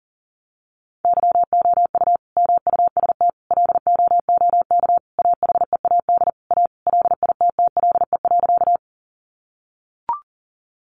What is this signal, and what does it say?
Morse code, heard as a keyed tone, about 37 wpm, 700 Hz. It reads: YOU MUST LOOK AHEAD A LITTLE.
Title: Sentences from Top 1100 Words 40wpm